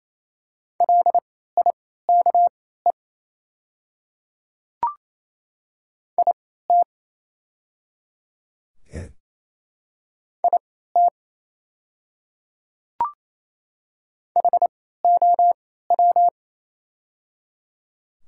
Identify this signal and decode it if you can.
Morse code, heard as a keyed tone, about 27 wpm, spaced as for 16 wpm, 700 Hz. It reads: LIKE IT IT HOW
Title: 3x Character Spacing - Top 100 Words - 28wpm